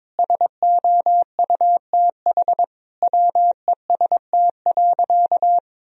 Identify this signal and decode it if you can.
Morse code, heard as a keyed tone, about 21 wpm, 700 Hz. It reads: SOUTH WEST.
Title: Sentences from Top 1000 Words - Review Part 2 - 22wpm